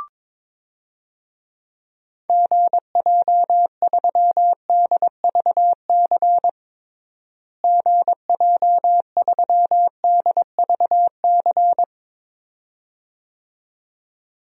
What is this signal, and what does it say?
Morse code, heard as a keyed tone, about 22 wpm, 700 Hz. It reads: GJ3DVC GJ3DVC
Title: DXCC-100 Call Signs - 6 Characters - Repeated 2x - 22wpm